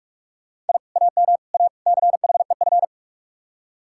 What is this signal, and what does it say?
Morse code, heard as a keyed tone, about 59 wpm, 700 Hz. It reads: I AM A CHEF